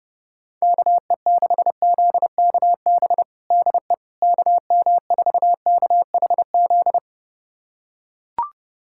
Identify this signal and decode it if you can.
Morse code, heard as a keyed tone, about 32 wpm, 700 Hz. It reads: KE6ZKB DE KM4KHZ
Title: USA Callsign DE Callsign 30wpm